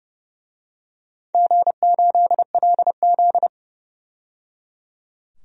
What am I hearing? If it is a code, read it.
Morse code, heard as a keyed tone, about 30 wpm, 700 Hz. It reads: G8LZ
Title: DXCC-100 Call Signs - 4 Characters - 30wpm